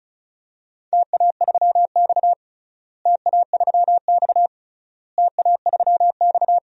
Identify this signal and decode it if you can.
Morse code, heard as a keyed tone, about 36 wpm, 700 Hz. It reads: TA3X TA3X TA3X
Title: DXCC-100 Call Signs - 4 Characters - Repeated 3x - 35wpm